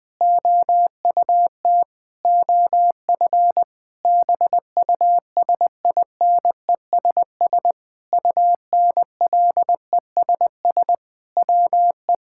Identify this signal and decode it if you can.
Morse code, heard as a keyed tone, about 20 wpm, 700 Hz. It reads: OUT OF BUSINESS UNLESS WE
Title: Sentences from Top 1000 Words - Review Part 6 - 20wpm